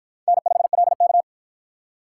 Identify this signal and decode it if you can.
Morse code, heard as a keyed tone, about 50 wpm, 700 Hz. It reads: N5LK